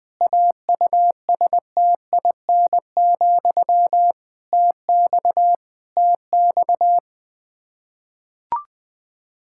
Morse code, heard as a keyed tone, about 21 wpm, 700 Hz. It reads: AUSTIN, TX TX
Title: K1USN SST QSO Practice 20wpm